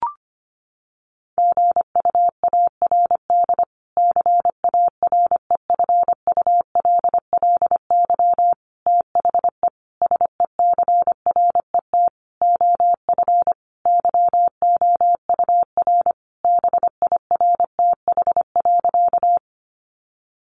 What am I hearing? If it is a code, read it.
Morse code, heard as a keyed tone, about 25 wpm, 700 Hz. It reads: GUARD CAREFULLY THE SECRET OF YOUR BIRTH.
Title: Sentences from Top 1300 Words 25wpm